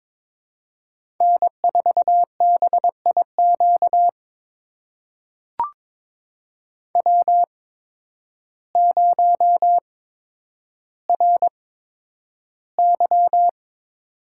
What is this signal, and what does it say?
Morse code, heard as a keyed tone, about 22 wpm, spaced as for 10 wpm, 700 Hz. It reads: N4BIQ W0RY
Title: USA Call Signs - 8x Character Spacing - 22wpm